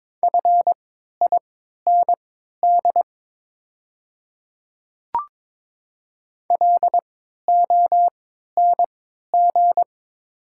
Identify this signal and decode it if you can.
Morse code, heard as a keyed tone, about 22 wpm, spaced as for 12 wpm, 690 Hz. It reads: FIND LONG
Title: 3x Character Spacing - Top 100 Words - 22wpm